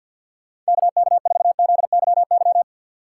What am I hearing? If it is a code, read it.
Morse code, heard as a keyed tone, about 60 wpm, 700 Hz. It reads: KK4ZYY
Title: Speed Racing - USA 2x3 Call Signs 50wpm